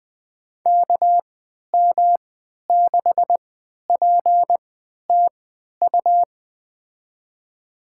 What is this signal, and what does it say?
Morse code, heard as a keyed tone, about 20 wpm, spaced as for 11 wpm, 700 Hz. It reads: KM6PTU
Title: USA Call Signs - 6 Characters - 3x Character Spacing - 20wpm